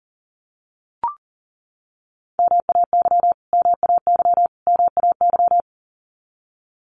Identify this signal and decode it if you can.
Morse code, heard as a keyed tone, about 40 wpm, 700 Hz. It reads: MAY MAY MAY
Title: Top 300 Words Repeated 40wpm